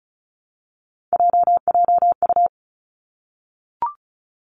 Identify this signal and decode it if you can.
Morse code, heard as a keyed tone, about 36 wpm, 700 Hz. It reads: JJU
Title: Three Challenging Letter-Number Combinations 35wpm